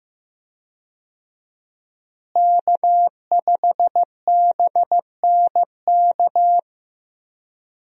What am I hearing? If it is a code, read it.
Morse code, heard as a keyed tone, about 15 wpm, 700 Hz. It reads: K5BNK
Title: USA Call Signs - 5 Characters - Repeated 3x - 15wpm